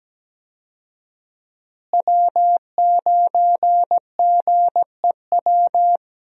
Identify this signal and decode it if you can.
Morse code, heard as a keyed tone, about 17 wpm, 700 Hz. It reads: W9GEW
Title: USA Call Signs - 5 Characters - Repeated 2x - 17wpm